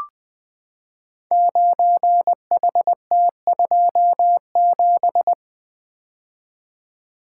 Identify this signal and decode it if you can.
Morse code, heard as a keyed tone, about 20 wpm, 700 Hz. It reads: 9HT27
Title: Code Groups - 5 Letter:Numbers - 20wpm